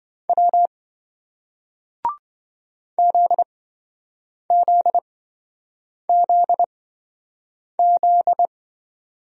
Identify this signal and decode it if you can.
Morse code, heard as a keyed tone, about 28 wpm, spaced as for 7 wpm, 700 Hz. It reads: W ZZZZ